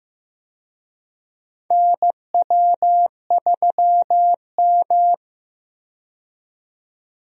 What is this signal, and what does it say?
Morse code, heard as a keyed tone, about 15 wpm, 700 Hz. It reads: NW3M
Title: USA Call Signs - 4 Characters - 15wpm